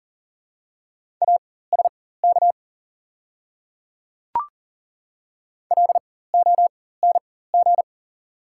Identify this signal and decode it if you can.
Morse code, heard as a keyed tone, about 38 wpm, spaced as for 18 wpm, 690 Hz. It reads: ASK LONG
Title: 4x Character Spacing - Top 300 Words - 40wpm